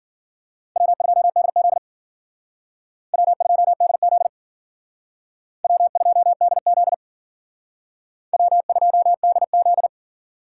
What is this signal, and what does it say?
Morse code, heard as a keyed tone, about 51 wpm, 700 Hz. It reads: W2DZ W2DZ W2DZ W2DZ